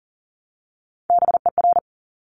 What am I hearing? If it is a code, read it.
Morse code, heard as a keyed tone, about 40 wpm, 700 Hz. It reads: BER
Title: Top Three Letter Combinations in Words 40wpm